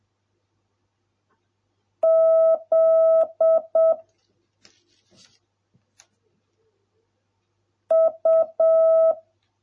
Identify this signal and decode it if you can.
Morse code, heard as a keyed tone, about 7 wpm, 650 Hz. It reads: Z U